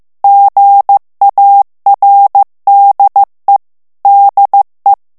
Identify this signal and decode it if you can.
Morse code, heard as a keyed tone, about 15 wpm, 800 Hz. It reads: GARDE DE
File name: FRANCAIS TEXTE CLAIR 2 VITESSE 14 MOTS MINUTES